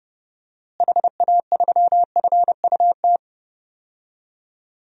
Morse code, heard as a keyed tone, about 29 wpm, 700 Hz. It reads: HA3FUT
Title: DXCC-100 Call Signs - 6 Characters - Repeated 2x - 30wpm